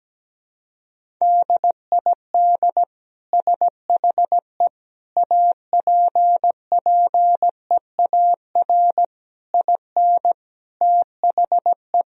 Morse code, heard as a keyed tone, about 17 wpm, 700 Hz. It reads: DID SHE APPEAR IN THE
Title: Sentences from Top 1000 Words - Review Part 4 - 17wpm